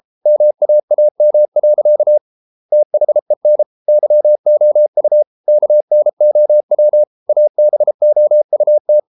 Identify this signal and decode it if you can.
Morse code, heard as a keyed tone, about 33 wpm, 600 Hz. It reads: MAAM. THEN YOU KNOW ABOUT